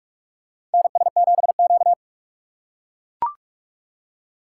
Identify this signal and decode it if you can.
Morse code, heard as a keyed tone, about 43 wpm, 700 Hz. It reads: NS7Q
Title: USA Call Signs - 4 Characters - 2x Character Spacing - 45wpm